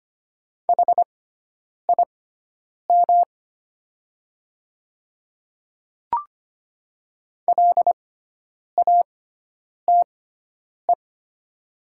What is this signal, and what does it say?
Morse code, heard as a keyed tone, about 25 wpm, spaced as for 9 wpm, 700 Hz. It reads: HIM LATE
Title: 6x Character Spacing - Top 300 Words - 25wpm